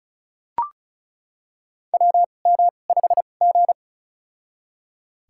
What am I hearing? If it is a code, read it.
Morse code, heard as a keyed tone, about 34 wpm, spaced as for 25 wpm, 700 Hz. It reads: WM5G
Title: USA Call Signs - 2x Character Spacing - 35wpm